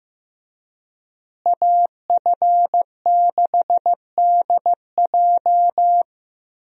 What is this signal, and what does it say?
Morse code, heard as a keyed tone, about 15 wpm, 700 Hz. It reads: AF6DJ
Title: USA Call Signs - 5 Characters - Repeated 3x - 15wpm